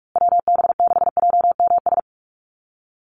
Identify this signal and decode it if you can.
Morse code, heard as a keyed tone, about 44 wpm, 700 Hz. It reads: WB6JMS
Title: USA Call Signs Repeated 2x 45wpm